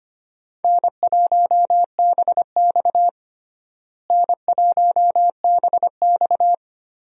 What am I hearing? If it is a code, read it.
Morse code, heard as a keyed tone, about 25 wpm, 700 Hz. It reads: N1BX N1BX
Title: USA Call Signs - 4 Characters - Repeated 3x - 25wpm